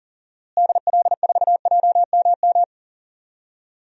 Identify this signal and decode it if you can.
Morse code, heard as a keyed tone, about 40 wpm, 700 Hz. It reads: DL4JMM
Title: DXCC-100 Call Signs - 6 Characters - Repeated 2x - 40wpm